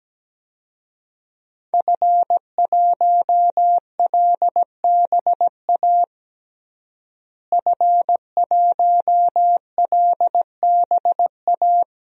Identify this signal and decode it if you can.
Morse code, heard as a keyed tone, about 17 wpm, 700 Hz. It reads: F1LBA F1LBA
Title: DXCC-100 Call Signs - 5 Characters - Repeated 2x - 17wpm